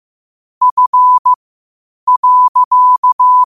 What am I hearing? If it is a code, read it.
Morse code, heard as a keyed tone, about 15 wpm, spaced as for 8 wpm, 1000 Hz. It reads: F.
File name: qotdfiles / 8wpm